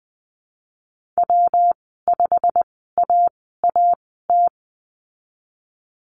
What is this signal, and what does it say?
Morse code, heard as a keyed tone, about 20 wpm, spaced as for 15 wpm, 700 Hz. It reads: W5AAT
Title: 154 Lesson 25g - Letter p - Call Signs 20wpm